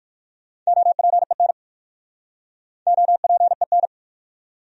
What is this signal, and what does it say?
Morse code, heard as a keyed tone, about 55 wpm, 700 Hz. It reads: OPEN OPEN
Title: Speed-Racing Top 300 Words 45wpm